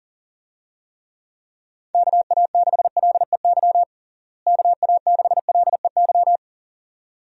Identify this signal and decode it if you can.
Morse code, heard as a keyed tone, about 40 wpm, 700 Hz. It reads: KA6LEY KA6LEY
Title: USA Call Signs - 6 Characters - Repeated 2x - 40wpm